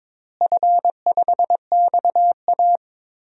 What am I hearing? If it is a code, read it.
Morse code, heard as a keyed tone, about 22 wpm, 700 Hz. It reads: F5XA
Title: DXCC-100 Call Signs - 4 Characters - Repeated 3x - 22wpm